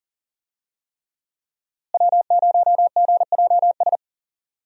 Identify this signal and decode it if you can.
Morse code, heard as a keyed tone, about 40 wpm, 700 Hz. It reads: W0GJS
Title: USA Call Signs - 5 Characters - Repeated 3x - 40wpm